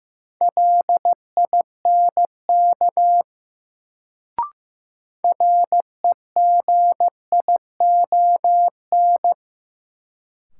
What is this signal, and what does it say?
Morse code, heard as a keyed tone, about 15 wpm, 700 Hz. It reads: LINK REGION
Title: Sentences from Top 1500 Words - New Word Drill Alt - 15wpm